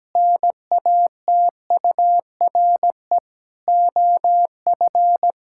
Morse code, heard as a keyed tone, about 17 wpm, 700 Hz. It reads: NATURE OF
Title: Sentences from Top 1000 Words - Review Part 5 - 17wpm